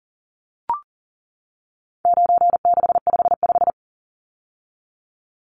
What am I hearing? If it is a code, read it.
Morse code, heard as a keyed tone, about 40 wpm, 700 Hz. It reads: 9655
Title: Numbers - 4-Digit 40wpm